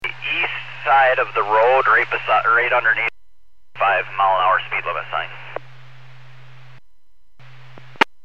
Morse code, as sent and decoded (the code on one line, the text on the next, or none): none